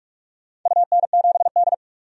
Morse code, heard as a keyed tone, about 47 wpm, 700 Hz. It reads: UN7D
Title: DXCC-100 Call Signs - 4 Characters - 45wpm